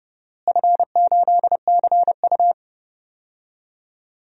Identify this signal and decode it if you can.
Morse code, heard as a keyed tone, about 30 wpm, 700 Hz. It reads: F8CU
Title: DXCC-100 Call Signs - 4 Characters - Repeated 2x - 30wpm